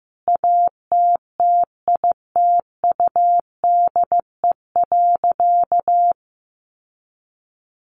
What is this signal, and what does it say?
Morse code, heard as a keyed tone, about 15 wpm, 700 Hz. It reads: ATTITUDE.
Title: Sentences from Top 1300 Words 15wpm